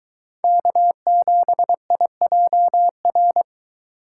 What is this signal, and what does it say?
Morse code, heard as a keyed tone, about 23 wpm, 700 Hz. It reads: K7IJR